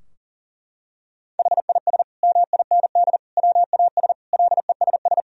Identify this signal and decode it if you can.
Morse code, heard as a keyed tone, about 37 wpm, 700 Hz. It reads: HIS MIND WAS LESS
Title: Sentences from Top 1000 Words - Review Part 2 - 40wpm